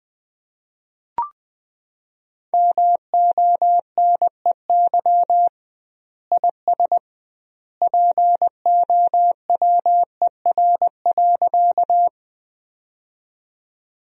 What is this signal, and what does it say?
Morse code, heard as a keyed tone, about 20 wpm, 700 Hz. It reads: MONEY IS POWER.
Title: 2x Word-Spacing - Sentences from Top 500 Words - 20wpm